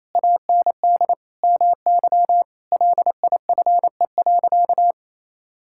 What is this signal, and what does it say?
Morse code, heard as a keyed tone, about 28 wpm, 700 Hz. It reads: AND MY LIFE.
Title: Sentences from Top 1000 Words - Review Part 5 - 28wpm